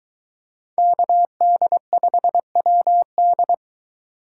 Morse code, heard as a keyed tone, about 23 wpm, 700 Hz. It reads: KD5WD